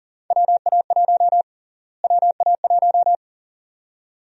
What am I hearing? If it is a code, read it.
Morse code, heard as a keyed tone, about 39 wpm, 700 Hz. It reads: WA1 WA1
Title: USA Call Sign Prefixes - Repeated 2x - 40wpm